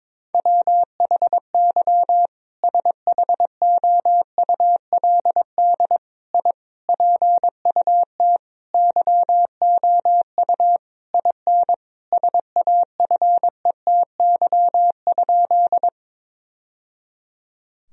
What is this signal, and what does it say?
Morse code, heard as a keyed tone, about 22 wpm, 700 Hz. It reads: WHY SHOULD I PUT YOU IN SAFETY?
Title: Sentences from Top 1000 Words - Review Part 4 - 22wpm